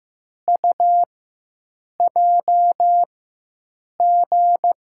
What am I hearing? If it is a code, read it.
Morse code, heard as a keyed tone, about 15 wpm, spaced as for 7 wpm, 700 Hz. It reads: UJG
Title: USA Call Signs - 6 Characters - 4x Character Spacing - 15wpm